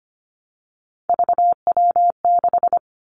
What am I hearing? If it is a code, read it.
Morse code, heard as a keyed tone, about 25 wpm, 700 Hz. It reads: VW6
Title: Three Challenging Letter-Number Combinations 25wpm